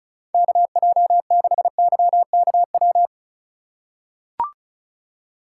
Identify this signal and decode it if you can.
Morse code, heard as a keyed tone, about 34 wpm, 690 Hz. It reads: KJ6YKW
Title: USA Call Signs - 6 Characters - 4x Character Spacing - 35wpm